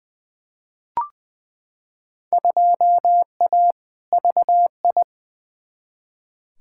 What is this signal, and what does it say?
Morse code, heard as a keyed tone, about 20 wpm, 700 Hz. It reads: 2A VI
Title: Field Day Exchanges - 20wpm